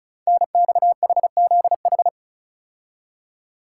Morse code, heard as a keyed tone, about 36 wpm, 690 Hz. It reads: NXHZH
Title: Code Groups - 5 Letters - 35wpm